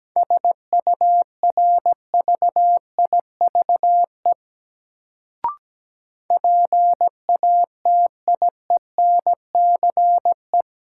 Morse code, heard as a keyed tone, about 17 wpm, 700 Hz. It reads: SURVIVE PATIENCE
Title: Sentences from Top 1300 Words - New Word Drill Alt - 17wpm